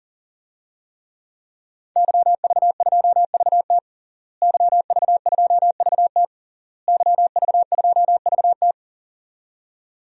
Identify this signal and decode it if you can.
Morse code, heard as a keyed tone, about 40 wpm, 700 Hz. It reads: YV2VT YV2VT YV2VT
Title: DXCC-100 Call Signs - 5 Characters - Repeated 3x - 40wpm